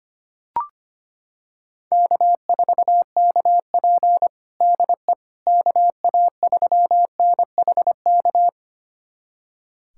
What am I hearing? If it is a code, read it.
Morse code, heard as a keyed tone, about 25 wpm, 700 Hz. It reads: K4KP DE KA3NHK
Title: USA Callsign DE Callsign 25wpm